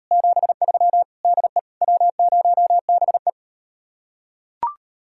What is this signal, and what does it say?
Morse code, heard as a keyed tone, about 38 wpm, 700 Hz. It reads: 73 DE W0BE